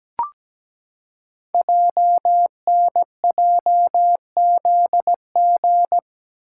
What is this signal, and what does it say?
Morse code, heard as a keyed tone, about 17 wpm, 700 Hz. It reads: JNJZG
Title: Code Groups - 5 Letters - 17wpm